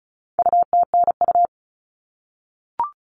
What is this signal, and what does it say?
Morse code, heard as a keyed tone, about 34 wpm, 700 Hz. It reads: UTNU
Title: All Cut Numbers - 4-Digit 35wpm